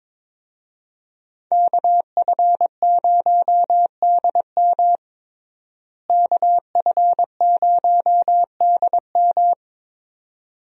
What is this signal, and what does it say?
Morse code, heard as a keyed tone, about 22 wpm, 700 Hz. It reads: KF0DM KF0DM
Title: USA Call Signs - 5 Characters - Repeated 2x - 22wpm